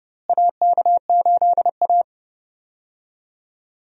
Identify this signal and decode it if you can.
Morse code, heard as a keyed tone, about 30 wpm, 700 Hz. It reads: AK8A
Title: USA Call Signs - 4 Characters - Repeated 3x - 30wpm